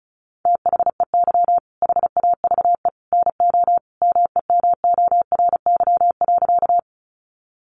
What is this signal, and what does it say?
Morse code, heard as a keyed tone, about 34 wpm, 700 Hz. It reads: THEY HAVE NO MEMORY.
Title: Sentences from Top 1300 Words 35wpm